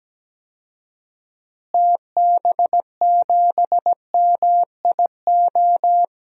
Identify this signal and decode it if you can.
Morse code, heard as a keyed tone, about 17 wpm, 700 Hz. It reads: TB7MIO
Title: DXCC-100 Call Signs - 6 Characters - Repeated 2x - 17wpm